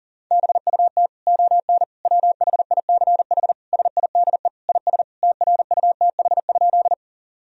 Morse code, heard as a keyed tone, about 37 wpm, 700 Hz. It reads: BUT ON WHICH SIDE IS TRUTH?
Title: Sentences from Top 1000 Words - Review Part 1 - 40wpm